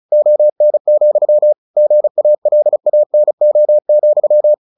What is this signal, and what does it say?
Morse code, heard as a keyed tone, about 34 wpm, 600 Hz. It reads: ON, GALANO,
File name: the old man and the sea 2 35wpm